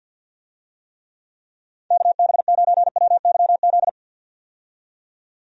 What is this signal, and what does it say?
Morse code, heard as a keyed tone, about 51 wpm, 700 Hz. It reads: KB9WYZ
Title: USA Call Signs - 6 Characters - 50wpm